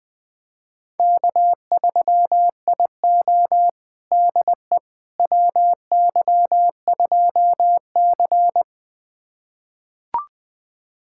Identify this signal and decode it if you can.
Morse code, heard as a keyed tone, about 20 wpm, 700 Hz. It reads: K3IO DE WY2C
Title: USA Callsign DE Callsign - 4 Characters - 20wpm